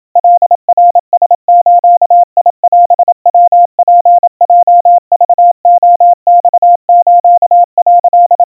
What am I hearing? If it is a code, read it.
Morse code (Japanese Wabun code), heard as a keyed tone, about 27 wpm, 700 Hz. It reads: カナラズオヤツヲクレマス」
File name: kawa / Morse sample / kamyu 100 cpm 700 Hz Crispy